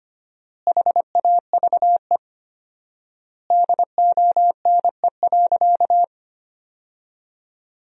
Morse code, heard as a keyed tone, about 25 wpm, 700 Hz. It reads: HAVE DONE.
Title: 4x Word-Spacing - Sentences from Top 500 Words - 25wpm